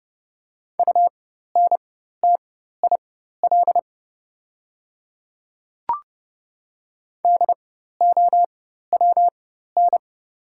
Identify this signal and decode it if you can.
Morse code, heard as a keyed tone, about 30 wpm, spaced as for 14 wpm, 690 Hz. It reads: UNTIL DOWN
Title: 4x Character Spacing - Top 500 Words - 30wpm